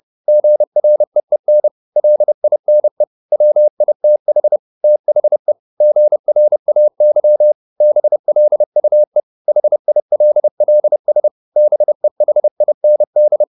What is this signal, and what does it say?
Morse code, heard as a keyed tone, about 30 wpm, 600 Hz. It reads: GREEN LINE WITH THE GRAY BLUE HILLS BEHIND
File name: the old man and the sea / the old man and the sea 1 30wpm